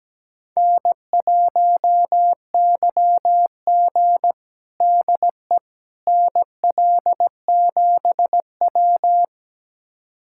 Morse code, heard as a keyed tone, about 17 wpm, 700 Hz. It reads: N1YG DE NL7W
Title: USA Callsign DE Callsign - 4 Characters - 17wpm